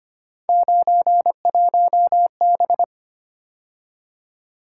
Morse code, heard as a keyed tone, about 29 wpm, 700 Hz. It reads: 91B
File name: Speed-Racing - Three Letter-Number Combinations - 25wpm